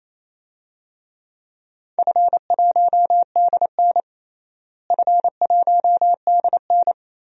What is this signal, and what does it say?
Morse code, heard as a keyed tone, about 27 wpm, 700 Hz. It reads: F1DN F1DN
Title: DXCC-100 Call Signs - 4 Characters - Repeated 3x - 28wpm